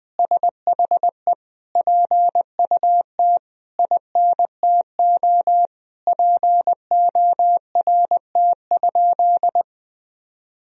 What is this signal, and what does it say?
Morse code, heard as a keyed tone, about 21 wpm, 700 Hz. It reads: SHE PUT INTO PORT?
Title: Sentences from Top 1000 Words - Review Part 1 - 20wpm